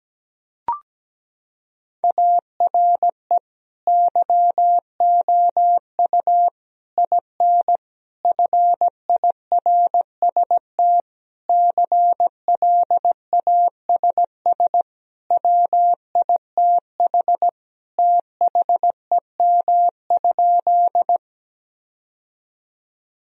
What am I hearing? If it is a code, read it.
Morse code, heard as a keyed tone, about 17 wpm, 700 Hz. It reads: ARE YOU IN FIRST CLASS WITH THEM?
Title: Sentences from Top 1000 Words - Review Part 2 - 17wpm